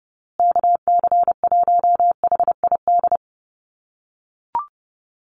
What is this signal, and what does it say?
Morse code, heard as a keyed tone, about 30 wpm, 700 Hz. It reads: KC1HID
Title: USA Call Signs Repeated 3x 30wpm